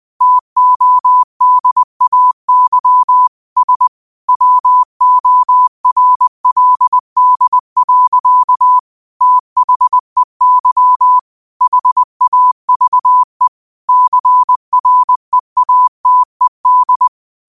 Morse code, heard as a keyed tone, about 20 wpm, 1000 Hz. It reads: TODAY S WORLD. THEY HAVE CREATED
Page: New quotes every day in morse code at 20 Words per minute.